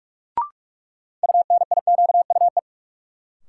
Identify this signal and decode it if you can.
Morse code, heard as a keyed tone, about 46 wpm, 700 Hz. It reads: UNIQUE
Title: Sentences from Top 1000 Words - Review Part 7 - New Word Drill - 45wpm